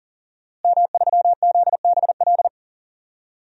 Morse code, heard as a keyed tone, about 40 wpm, 700 Hz. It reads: M3ZBL
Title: DXCC-100 Call Signs - 5 Characters - Repeated 2x - 40wpm